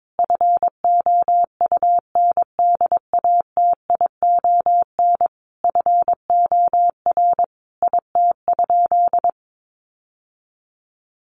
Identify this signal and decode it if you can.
Morse code, heard as a keyed tone, about 22 wpm, 700 Hz. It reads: FOUNDATION FOR IT?
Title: Sentences from Top 1200 Words 22wpm